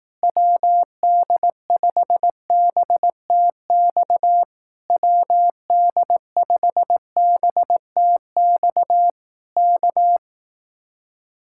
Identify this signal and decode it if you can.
Morse code, heard as a keyed tone, about 18 wpm, 700 Hz. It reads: WD5BTX WD5BTX K
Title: Speed-Racing Common QSO Phrases Encore - 15wpm